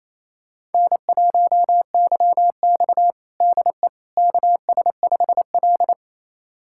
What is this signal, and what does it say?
Morse code, heard as a keyed tone, about 27 wpm, 700 Hz. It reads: N1YX DE KS5L
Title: USA Callsign DE Callsign - 4 Characters - 28wpm